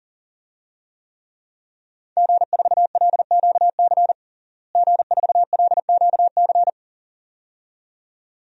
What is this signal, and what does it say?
Morse code, heard as a keyed tone, about 40 wpm, 700 Hz. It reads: G4LQC G4LQC
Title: DXCC-100 Call Signs - 5 Characters - Repeated 2x - 40wpm